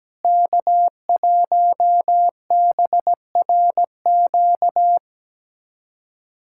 Morse code, heard as a keyed tone, about 17 wpm, 700 Hz. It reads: K1BRQ
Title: USA Callsign DE Callsign 17wpm